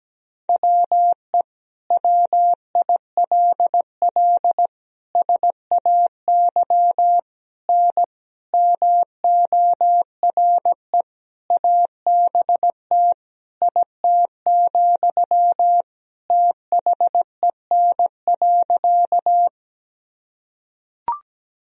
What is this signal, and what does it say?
Morse code, heard as a keyed tone, about 17 wpm, 700 Hz. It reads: WE WILL SAY N MORE ABT IT, THEN.
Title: Sentences from Top 200 Words and Abbreviations 17wpm